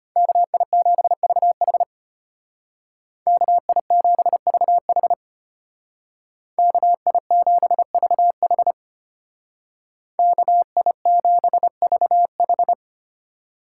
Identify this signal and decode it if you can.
Morse code, heard as a keyed tone, about 30 wpm, 700 Hz. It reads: KI7VH KI7VH KI7VH KI7VH
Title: Speed Racing - USA 2x2 Call Signs 25wpm